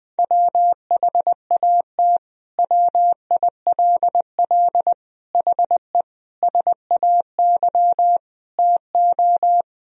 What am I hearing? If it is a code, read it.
Morse code, heard as a keyed tone, about 20 wpm, 700 Hz. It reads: WHAT WILL HE SAY TO
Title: Sentences from Top 1000 Words - Review Part 3 - 20wpm